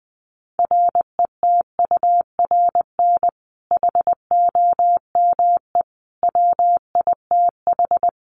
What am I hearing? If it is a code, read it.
Morse code, heard as a keyed tone, about 20 wpm, 700 Hz. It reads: RETURN HOME WITH
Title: Sentences from Top 800 Words 20wpm